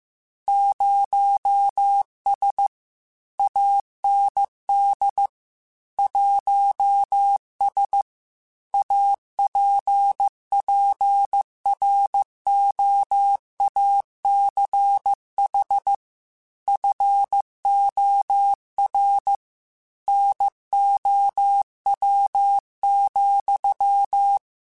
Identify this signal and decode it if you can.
Morse code, heard as a keyed tone, about 15 wpm, 800 Hz. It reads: 0S AND 1S APPROACH FOR NOW,
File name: Technical Subjects / Technical Subjects-mp3s / Gates Intro @13